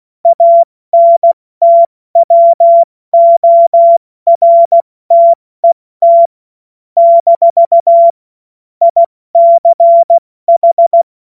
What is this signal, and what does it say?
Morse code, heard as a keyed tone, about 16 wpm, 680 Hz. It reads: ANTWORTET - ICH
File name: Morsekurs / Witze / Witz 06. April 2021, 70 F80